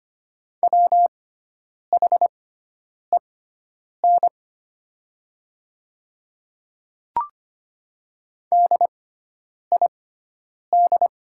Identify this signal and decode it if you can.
Morse code, heard as a keyed tone, about 25 wpm, spaced as for 9 wpm, 700 Hz. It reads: WHEN DID
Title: 6x Character Spacing - Top 100 Words - 25wpm